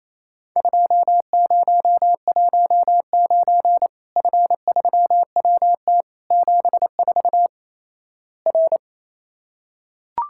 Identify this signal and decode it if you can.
Morse code, heard as a keyed tone, about 28 wpm, 700 Hz. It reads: 2019 F3WT 74 R
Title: Worked All Europe DX Contest - QTC Exchange - 28wpm